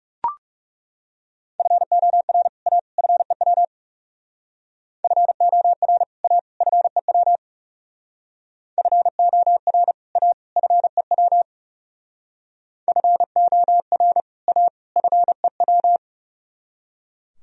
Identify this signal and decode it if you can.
Morse code, heard as a keyed tone, about 38 wpm, 700 Hz. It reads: FOR A FEW FOR A FEW FOR A FEW FOR A FEW